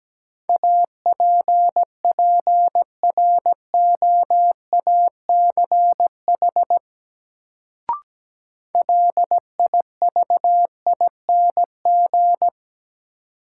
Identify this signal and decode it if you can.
Morse code, heard as a keyed tone, about 17 wpm, 700 Hz. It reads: APPROACH LIVING
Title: Sentences from Top 1000 Words - Review Part 6 - New Word Drill - 17wpm